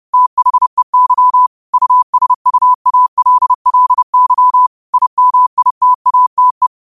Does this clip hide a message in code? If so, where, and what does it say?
Yes: Morse code from 0.1 to 6.7 s, heard as a keyed tone, about 30 wpm, 1000 Hz. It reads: THEY USUALLY IMITATE